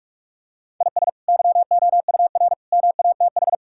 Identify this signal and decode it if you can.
Morse code, heard as a keyed tone, about 46 wpm, 700 Hz. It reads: IS YOUR MATH
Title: Sentences from Top 1000 Words - Review Part 6 - 45wpm